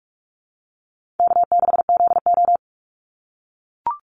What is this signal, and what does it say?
Morse code, heard as a keyed tone, about 44 wpm, 700 Hz. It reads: K6ZO
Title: Easy Call Signs - 45wpm